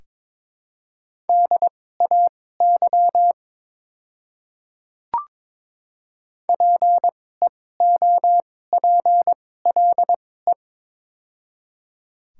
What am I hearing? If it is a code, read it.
Morse code, heard as a keyed tone, about 23 wpm, spaced as for 16 wpm, 700 Hz. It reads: DAY PEOPLE
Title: 2x Character Spacing - Top 100 Words - 22wpm